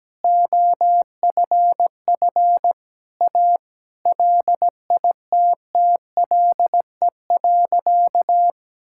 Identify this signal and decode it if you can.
Morse code, heard as a keyed tone, about 17 wpm, 700 Hz. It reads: OFF A LITTLE.
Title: Sentences from Top 1000 Words - Review Part 6 - 17wpm